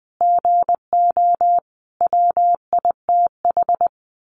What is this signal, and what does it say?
Morse code, heard as a keyed tone, about 20 wpm, 700 Hz. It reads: GO WITH
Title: Sentences from Top 1300 Words 20wpm